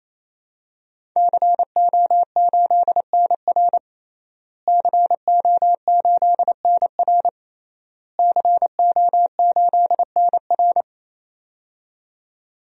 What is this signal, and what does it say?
Morse code, heard as a keyed tone, about 28 wpm, 700 Hz. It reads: CO8NR CO8NR CO8NR
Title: DXCC-100 Call Signs - 5 Characters - Repeated 3x - 28wpm